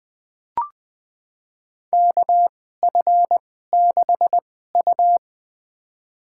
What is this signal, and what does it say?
Morse code, heard as a keyed tone, about 20 wpm, spaced as for 14 wpm, 700 Hz. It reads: KF6U
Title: USA Call Signs - 4 Characters - 2x Character Spacing - 20wpm